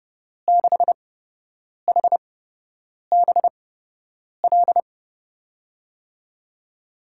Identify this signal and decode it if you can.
Morse code, heard as a keyed tone, about 32 wpm, spaced as for 8 wpm, 700 Hz. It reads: 6HBL
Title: USA Call Signs - 6 Characters - 8x Character Spacing - 30wpm